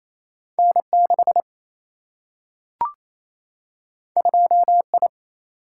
Morse code, heard as a keyed tone, about 28 wpm, 700 Hz. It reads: N6 2I
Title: Single-Letter Single-Number 28wpm